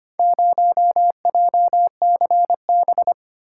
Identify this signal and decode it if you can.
Morse code, heard as a keyed tone, about 25 wpm, 700 Hz. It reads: 0JCB
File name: practice / 25wpm / Four Letter-Number Combinations - 25wpm